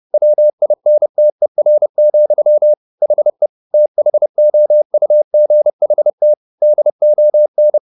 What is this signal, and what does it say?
Morse code, heard as a keyed tone, about 30 wpm, 600 Hz. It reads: WINTER, HE THOUGHT DON